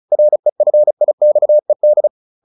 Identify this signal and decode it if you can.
Morse code, heard as a keyed tone, about 35 wpm, 600 Hz. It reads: REFIXED